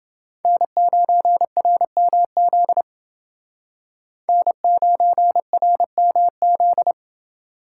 Morse code, heard as a keyed tone, about 28 wpm, 700 Hz. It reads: N9RMZ N9RMZ